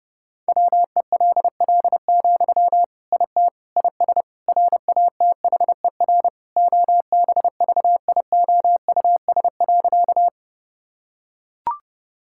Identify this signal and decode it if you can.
Morse code, heard as a keyed tone, about 28 wpm, 700 Hz. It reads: WELL, IT IS RATHER OBVIOUS.
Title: Sentences from Top 1600 Words 30wpm